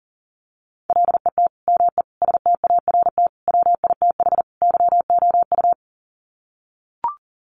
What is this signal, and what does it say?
Morse code, heard as a keyed tone, about 40 wpm, 700 Hz. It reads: LET ME START WITH YOU
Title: Sets of 5 Words Encore 40wpm